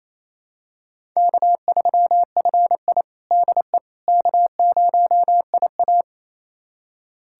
Morse code, heard as a keyed tone, about 28 wpm, 700 Hz. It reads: K3FI DE K0IA
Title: USA Callsign DE Callsign - 4 Characters - 28wpm